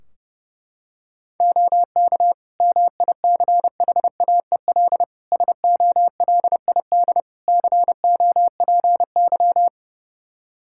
Morse code, heard as a keyed tone, about 32 wpm, 700 Hz. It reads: OK MICHAEL SOLID COPY
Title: Practice QSO - Exchange 3 - More Info - 30wpm